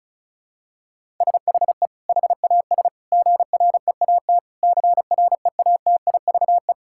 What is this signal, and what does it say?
Morse code, heard as a keyed tone, about 34 wpm, 700 Hz. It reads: SHE HAS GREAT CREATIVE
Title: Sentences from Top 1000 Words - Review Part 6 - 35wpm